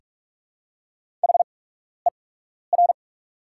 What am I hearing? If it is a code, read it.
Morse code, heard as a keyed tone, about 44 wpm, spaced as for 12 wpm, 690 Hz. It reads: HER
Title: 8x Character Spacing - Top 300 Words - 45wpm